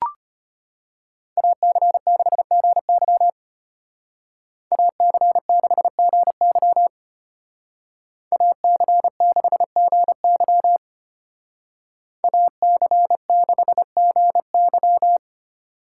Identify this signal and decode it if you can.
Morse code, heard as a keyed tone, about 33 wpm, 700 Hz. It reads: AC6GY AC6GY AC6GY AC6GY